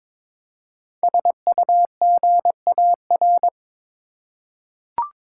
Morse code, heard as a keyed tone, about 22 wpm, 700 Hz. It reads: SUGAR
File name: 5 Letter Words - First 3 Letters Given - 22wpm